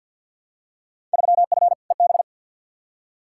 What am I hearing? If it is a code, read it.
Morse code, heard as a keyed tone, about 50 wpm, 700 Hz. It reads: 3F EB
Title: Field Day Exchanges - 50wpm